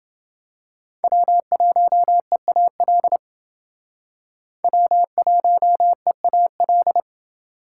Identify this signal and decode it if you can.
Morse code, heard as a keyed tone, about 29 wpm, 700 Hz. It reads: W1EAL W1EAL